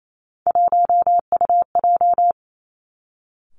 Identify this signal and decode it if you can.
Morse code, heard as a keyed tone, about 28 wpm, 700 Hz. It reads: 1UJ
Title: Three Challenging Letter-Number Combinations 28wpm